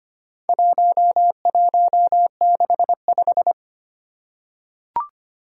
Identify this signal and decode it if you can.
Morse code, heard as a keyed tone, about 25 wpm, 700 Hz. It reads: 1165
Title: Four Challenging Letter-Number Combinations 25wpm